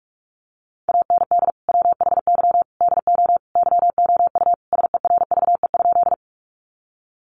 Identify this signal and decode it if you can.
Morse code, heard as a keyed tone, about 44 wpm, 700 Hz. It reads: AND WHY DO YOU SERVE?
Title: Sentences from Top 400 Words 45wpm